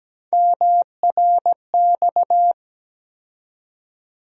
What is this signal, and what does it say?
Morse code, heard as a keyed tone, about 17 wpm, 700 Hz. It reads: MRX
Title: USA Call Sign Postfixes - 3 Letters 17wpm